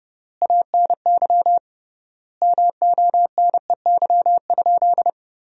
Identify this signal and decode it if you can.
Morse code, heard as a keyed tone, about 30 wpm, 700 Hz. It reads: ANY MONEY?
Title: 3x Word-Spacing - Sentences from Top 400 Words - 30wpm